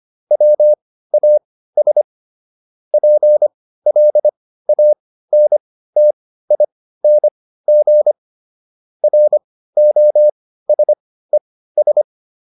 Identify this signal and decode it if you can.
Morse code, heard as a keyed tone, about 25 wpm, spaced as for 15 wpm, 600 Hz. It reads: WAS PLANTING ROSES